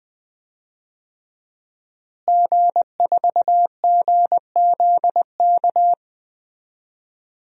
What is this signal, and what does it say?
Morse code, heard as a keyed tone, about 21 wpm, 700 Hz. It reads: G4GZK
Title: DXCC-100 Call Signs - 5 Characters - Repeated 2x - 20wpm